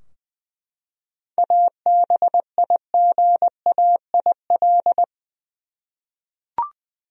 Morse code, heard as a keyed tone, about 20 wpm, 700 Hz. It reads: ABIGAIL
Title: USA Common Names - 20wpm